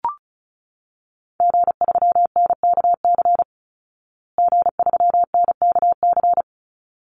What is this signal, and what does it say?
Morse code, heard as a keyed tone, about 34 wpm, 700 Hz. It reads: G3NKC G3NKC
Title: Call Signs Repeated 3x - 35wpm